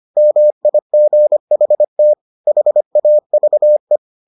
Morse code, heard as a keyed tone, about 25 wpm, 600 Hz. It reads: MIGHT HAVE